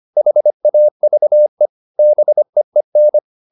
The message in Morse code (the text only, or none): HAVE BEEN